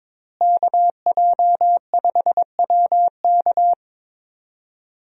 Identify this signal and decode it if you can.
Morse code, heard as a keyed tone, about 22 wpm, 700 Hz. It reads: KJ5WK
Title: USA Call Signs - 5 Characters - Repeated 2x - 22wpm